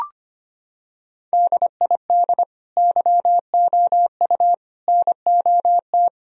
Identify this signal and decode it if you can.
Morse code, heard as a keyed tone, about 25 wpm, 700 Hz. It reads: DID YOU NOT
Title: Sentences from Top 1000 Words - Review Part 2 - 25wpm